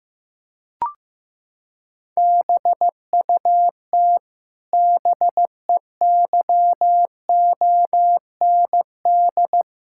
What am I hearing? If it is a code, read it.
Morse code, heard as a keyed tone, about 15 wpm, 700 Hz. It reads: BUT BEYOND
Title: Sentences from Top 1000 Words - Review Part 6 - 15wpm